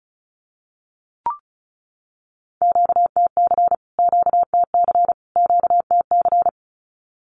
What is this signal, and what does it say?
Morse code, heard as a keyed tone, about 34 wpm, 700 Hz. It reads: QTC QTC QTC
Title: Q-Codes Repeated 3x 35wpm